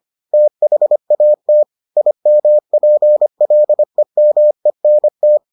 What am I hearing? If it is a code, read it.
Morse code, heard as a keyed tone, about 25 wpm, 600 Hz. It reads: THAT IMPLEMENT